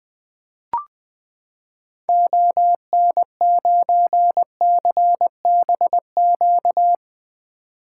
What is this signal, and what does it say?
Morse code, heard as a keyed tone, about 21 wpm, 700 Hz. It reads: ON9CBQ
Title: 2023 SOTA Activator Call Signs - 20wpm